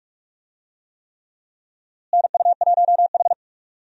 Morse code, heard as a keyed tone, about 48 wpm, 700 Hz. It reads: NU1H
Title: USA Call Signs - 4 Characters - Repeated 3x - 45wpm